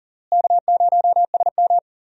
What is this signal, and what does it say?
Morse code, heard as a keyed tone, about 38 wpm, 700 Hz. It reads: K0SM
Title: USA Call Signs - 4 Characters - Repeated 2x - 40wpm